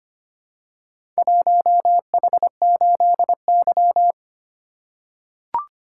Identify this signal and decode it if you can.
Morse code, heard as a keyed tone, about 26 wpm, 700 Hz. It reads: 1H8Y
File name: practice / 25wpm / Four Letter-Number Combinations - 25wpm